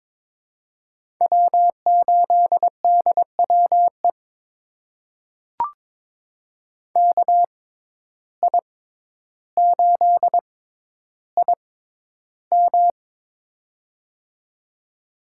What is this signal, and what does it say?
Morse code, heard as a keyed tone, about 22 wpm, spaced as for 11 wpm, 700 Hz. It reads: W8DWE KI8IM
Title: USA Call Signs - 5 Characters - 6x Character Spacing - 22wpm